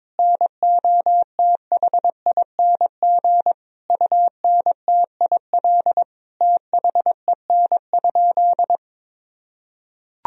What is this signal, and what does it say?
Morse code, heard as a keyed tone, about 22 wpm, 700 Hz. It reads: NOTHING UNTIL THEN?
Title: Sentences from Top 1000 Words - Review Part 1 - 22wpm